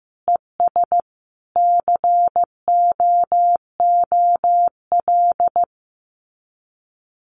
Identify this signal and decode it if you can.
Morse code, heard as a keyed tone, about 15 wpm, 700 Hz. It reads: ES COOL
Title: Common QSO Phrases 15wpm